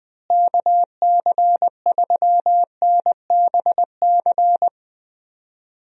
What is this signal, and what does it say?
Morse code, heard as a keyed tone, about 20 wpm, 700 Hz. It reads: KC3NBC
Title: USA Call Signs - 6 Characters - Repeated 2x - 20wpm